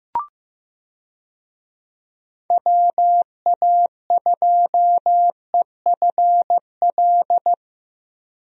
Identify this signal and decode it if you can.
Morse code, heard as a keyed tone, about 15 wpm, 700 Hz. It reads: WA2EFL
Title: USA Call Signs - 6 Characters - Repeated 3x - 15wpm